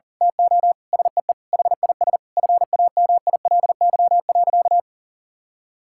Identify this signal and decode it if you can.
Morse code, heard as a keyed tone, about 37 wpm, 700 Hz. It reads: TO SEE HIS FAMILY.
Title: Sentences from Top 1000 Words - Review Part 6 - 40wpm